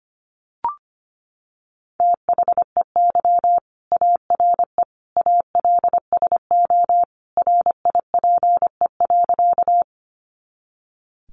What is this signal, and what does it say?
Morse code, heard as a keyed tone, about 25 wpm, 700 Hz. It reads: THEY ARE ALSO RIPE.
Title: Additional Easy Phrases 25wpm